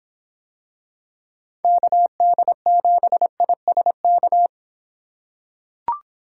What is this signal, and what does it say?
Morse code, heard as a keyed tone, about 26 wpm, 700 Hz. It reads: KD7ISK